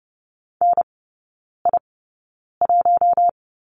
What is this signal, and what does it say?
Morse code, heard as a keyed tone, about 30 wpm, spaced as for 9 wpm, 700 Hz. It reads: NI1
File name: 081 Lesson 14e - Number 2 - Three Characters 30wpm